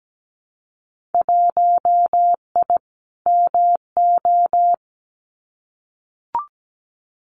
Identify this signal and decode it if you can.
Morse code, heard as a keyed tone, about 17 wpm, 700 Hz. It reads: 1I MO
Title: Winter Field Day Exchanges Encore 17wpm